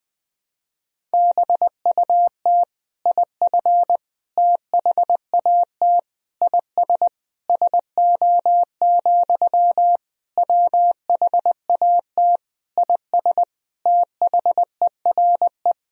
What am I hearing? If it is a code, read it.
Morse code, heard as a keyed tone, about 20 wpm, 700 Hz. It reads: BUT IF THAT IS SO, WHAT IS THERE
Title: Sentences from Top 200 Words and Abbreviations 20wpm